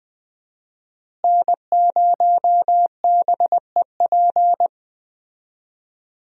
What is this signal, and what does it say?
Morse code, heard as a keyed tone, about 20 wpm, 700 Hz. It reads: N0BEP
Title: USA Call Signs - 5 Characters - Repeated 3x - 20wpm